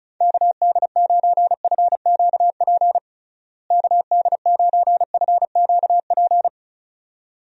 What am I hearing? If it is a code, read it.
Morse code, heard as a keyed tone, about 36 wpm, 700 Hz. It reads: KD9FQP KD9FQP
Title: USA Call Signs - 6 Characters - Repeated 2x - 35wpm